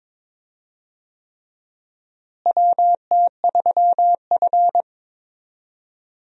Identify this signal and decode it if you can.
Morse code, heard as a keyed tone, about 22 wpm, 700 Hz. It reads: WT3F
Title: USA Call Signs - 4 Characters - Repeated 3x - 22wpm